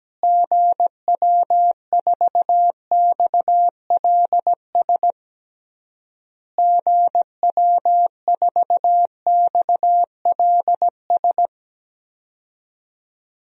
Morse code, heard as a keyed tone, about 17 wpm, 700 Hz. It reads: GW4XLS GW4XLS
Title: DXCC-100 Call Signs - 6 Characters - Repeated 3x - 17wpm